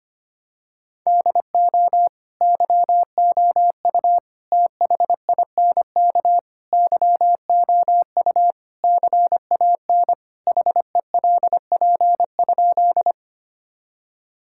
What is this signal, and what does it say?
Morse code, heard as a keyed tone, about 25 wpm, 700 Hz. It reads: DO YOU THINK YOU CAN HELP?
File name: practice / 25wpm / Sentences with 4-Word Phrases - 25wpm